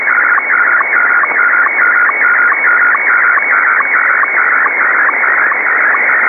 SSTV (slow-scan television), Scottie 1.